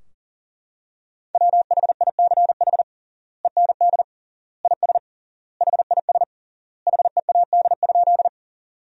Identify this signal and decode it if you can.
Morse code, heard as a keyed tone, about 40 wpm, 700 Hz. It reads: WHICH END IS HIS HEAD?
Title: 3x Word-Spacing - Sentences from Top 300 Words - 40wpm